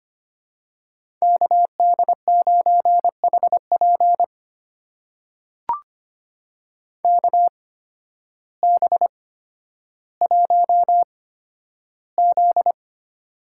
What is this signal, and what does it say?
Morse code, heard as a keyed tone, about 25 wpm, spaced as for 12 wpm, 700 Hz. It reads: KD9HP KB1Z
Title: USA Call Signs - 8x Character Spacing - 25wpm